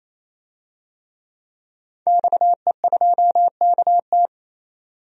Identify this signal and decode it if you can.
Morse code, heard as a keyed tone, about 28 wpm, 700 Hz. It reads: XE2KT
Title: DXCC-100 Call Signs - 5 Characters - 28wpm